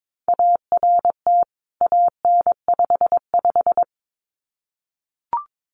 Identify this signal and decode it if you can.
Morse code, heard as a keyed tone, about 22 wpm, 700 Hz. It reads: ART AN55
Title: CWT Exchange 22wpm